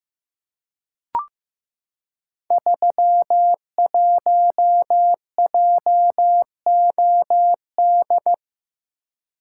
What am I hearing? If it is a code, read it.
Morse code, heard as a keyed tone, about 15 wpm, 700 Hz. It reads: 31JOD
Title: Code Groups - 5 Letter:Numbers - 15wpm